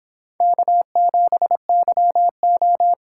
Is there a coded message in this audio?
Yes: Morse code from 0.4 to 2.9 s, heard as a keyed tone, about 26 wpm, 700 Hz. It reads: K7YO